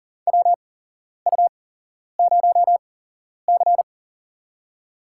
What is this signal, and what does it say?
Morse code, heard as a keyed tone, about 40 wpm, spaced as for 11 wpm, 700 Hz. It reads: WU0C
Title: USA Call Signs - 8x Character Spacing - 40wpm